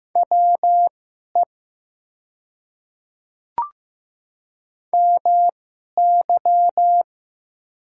Morse code, heard as a keyed tone, about 15 wpm, spaced as for 11 wpm, 700 Hz. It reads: WE MY
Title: 2x Character Spacing - Top 200 Words - 15wpm